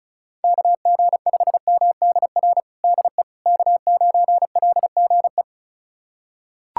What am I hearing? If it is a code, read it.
Morse code, heard as a keyed tone, about 36 wpm, 700 Hz. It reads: KG5MDR DE K9LGE
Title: USA Callsign DE Callsign 35wpm